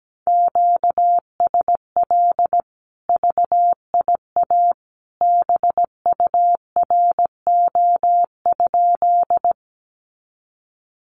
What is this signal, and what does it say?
Morse code, heard as a keyed tone, about 17 wpm, 700 Hz. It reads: QSL VIA BURO?
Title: Common QSO Phrases 17wpm